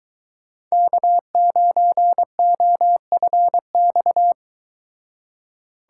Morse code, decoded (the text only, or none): K9OFX